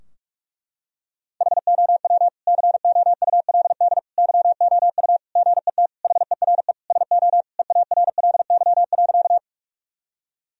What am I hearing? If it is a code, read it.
Morse code, heard as a keyed tone, about 46 wpm, 700 Hz. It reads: HOW COULD YOU GET HERE SO EARLY.
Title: Sentences from Top 1000 Words - Review Part 2 - 45wpm